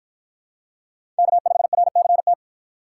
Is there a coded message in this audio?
Yes: Morse code from 1.2 to 2.3 s, heard as a keyed tone, about 49 wpm, 700 Hz. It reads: K5RCT